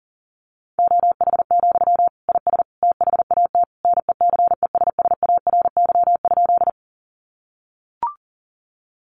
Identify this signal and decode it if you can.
Morse code, heard as a keyed tone, about 40 wpm, 700 Hz. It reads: OH, IS THAT NECESSARY?
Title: Sentences from Top 800 Words 40wpm